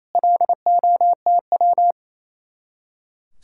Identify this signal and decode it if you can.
Morse code, heard as a keyed tone, about 34 wpm, 700 Hz. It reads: LOTW
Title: Speed-Racing Common QSO Phrases - 28wpm